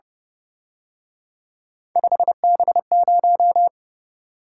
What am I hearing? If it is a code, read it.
Morse code, heard as a keyed tone, about 29 wpm, 700 Hz. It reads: 5B0
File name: Speed-Racing - Three Letter-Number Combinations - 25wpm